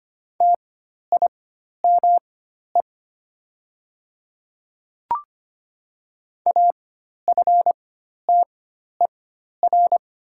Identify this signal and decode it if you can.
Morse code, heard as a keyed tone, about 25 wpm, spaced as for 12 wpm, 700 Hz. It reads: TIME AFTER
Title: 4x Character Spacing - Top 300 Words - 25wpm